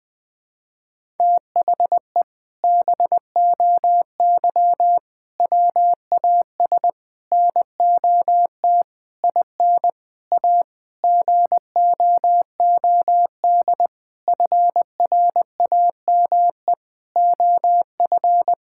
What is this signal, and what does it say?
Morse code, heard as a keyed tone, about 20 wpm, 700 Hz. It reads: THE BOY WAS NOT IN A GOOD FRAME OF
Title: Sentences from Top 1000 Words - Review Part 6 - 20wpm